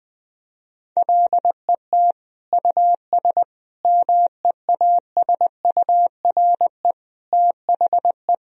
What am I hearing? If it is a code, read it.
Morse code, heard as a keyed tone, about 21 wpm, 700 Hz. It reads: LET US MEASURE THE
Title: Sentences from Top 1000 Words - Review Part 2 - 20wpm